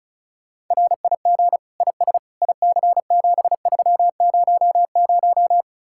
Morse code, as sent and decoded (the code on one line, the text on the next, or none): .-. .. --. / .. ... / .. -.-. --... ...-- ----- -----
RIG IS IC7300